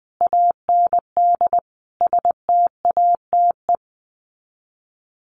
Morse code, heard as a keyed tone, about 20 wpm, 700 Hz. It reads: AND STATE
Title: Sets of 5 Words 20wpm